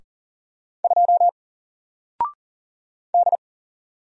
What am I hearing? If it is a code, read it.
Morse code, heard as a keyed tone, about 40 wpm, 700 Hz. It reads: 2 D
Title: Single Character (Letter:Number:Punctuation:Prosign) Alt 40wpm